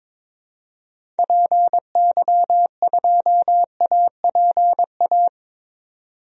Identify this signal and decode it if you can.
Morse code, heard as a keyed tone, about 22 wpm, 700 Hz. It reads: PY2APA
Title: DXCC-100 Call Signs - 6 Characters - 22wpm